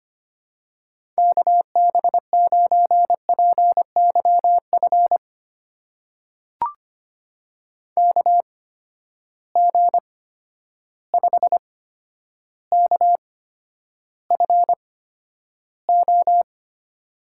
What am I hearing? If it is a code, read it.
Morse code, heard as a keyed tone, about 25 wpm, spaced as for 11 wpm, 700 Hz. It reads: KB9PYF KG5KFO
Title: USA Call Signs - 6 Characters - 8x Character Spacing - 25wpm